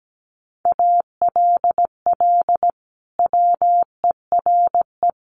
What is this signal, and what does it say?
Morse code, heard as a keyed tone, about 17 wpm, 700 Hz. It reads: ALL WERE
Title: Sentences from Top 1400 Words 17wpm